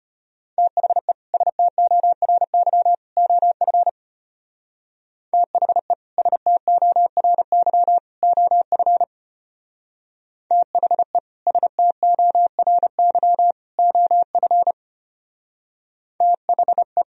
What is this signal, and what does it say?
Morse code, heard as a keyed tone, about 31 wpm, 700 Hz. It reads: THE STORY OF THE STORY OF THE STORY OF THE
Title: Speed-Racing - Sets of 3 Words 25wpm